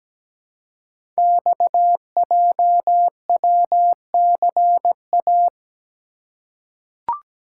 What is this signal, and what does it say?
Morse code, heard as a keyed tone, about 17 wpm, 700 Hz. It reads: XJWCA
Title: Code Groups - 5 Letter:Numbers - 17wpm